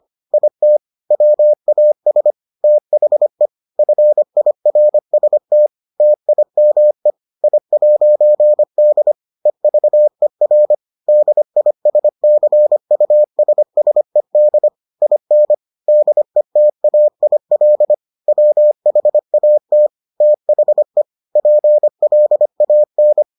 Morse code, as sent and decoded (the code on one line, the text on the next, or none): .. - / .-- .- ... / - .... / ..-. .. .-. ... - / - .. -- . / .. .----. -.. / . ...- . .-. / -.. .. ... -.-. ..- ... ... -.. / .. -. / -.. - .- .. .-.. / .-- .... .- - / - .... / .--. .-.. .- -.
IT WAS THE FIRST TIME I'D EVER DISCUSSED IN DETAIL WHAT THE PLAN